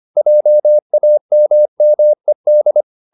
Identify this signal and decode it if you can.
Morse code, heard as a keyed tone, about 25 wpm, 600 Hz. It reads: JAMMED